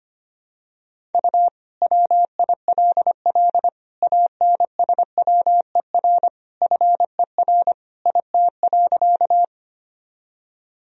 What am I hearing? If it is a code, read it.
Morse code, heard as a keyed tone, about 25 wpm, 700 Hz. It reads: U WILL ANSWER FER IT.
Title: Sentences from Top 300 Words and Abbreviations 25wpm